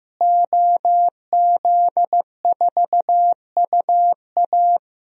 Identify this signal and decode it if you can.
Morse code, heard as a keyed tone, about 15 wpm, 700 Hz. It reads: OZ4UA
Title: DXCC-100 Call Signs - 5 Characters - Repeated 2x - 15wpm